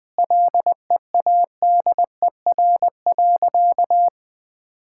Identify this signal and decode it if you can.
Morse code, heard as a keyed tone, about 20 wpm, 700 Hz. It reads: LEADER.
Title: Sentences with 2-Word Phrases - Spoken Beginning - 20wpm